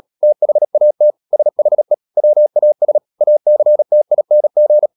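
Morse code, heard as a keyed tone, about 37 wpm, 600 Hz. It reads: THAT SHE WAS ACTING
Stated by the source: Morse practice at 36 wpm, text of Ken Follett's novel Never, from CW med Gnister.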